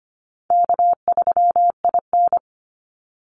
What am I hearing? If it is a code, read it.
Morse code, heard as a keyed tone, about 25 wpm, 700 Hz. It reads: K3IN
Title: Easy Call Signs Repeated 3x - 25wpm